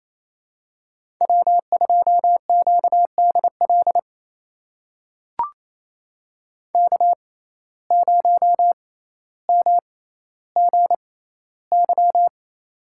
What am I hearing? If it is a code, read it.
Morse code, heard as a keyed tone, about 28 wpm, spaced as for 14 wpm, 700 Hz. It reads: W2QDL K0MGY
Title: USA Call Signs - 5 Characters - 6x Character Spacing - 28wpm